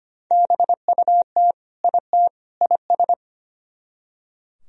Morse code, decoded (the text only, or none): BUT IT IS